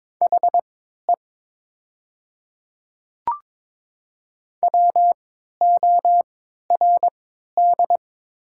Morse code, heard as a keyed tone, about 22 wpm, spaced as for 13 wpm, 700 Hz. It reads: HE WORD
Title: 3x Character Spacing - Top 100 Words - 22wpm